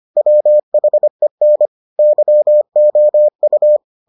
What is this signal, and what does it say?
Morse code, heard as a keyed tone, about 25 wpm, 600 Hz. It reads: WHEN YOU